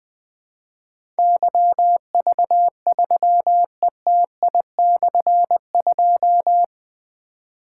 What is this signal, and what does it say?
Morse code, heard as a keyed tone, about 20 wpm, 700 Hz. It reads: YV3ETI/2
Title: Difficult Call Signs 20wpm